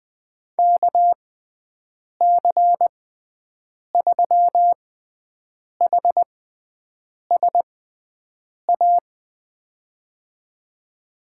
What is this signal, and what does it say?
Morse code, heard as a keyed tone, about 20 wpm, spaced as for 7 wpm, 700 Hz. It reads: KC3HSA
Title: USA Call Signs - 6 Characters - 6x Character Spacing - 20wpm